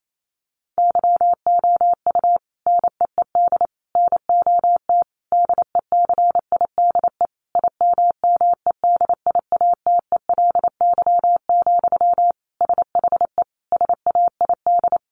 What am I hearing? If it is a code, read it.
Morse code, heard as a keyed tone, about 28 wpm, 700 Hz. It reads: YOU NEED NOT DECIDE IMMEDIATELY, SHE SAID
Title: Sentences from Top 1500 Words 28wpm